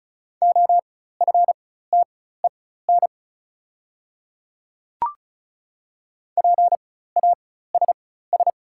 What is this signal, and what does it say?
Morse code, heard as a keyed tone, about 34 wpm, spaced as for 16 wpm, 690 Hz. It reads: OFTEN PASS
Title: 4x Character Spacing - Top 500 Words - 35wpm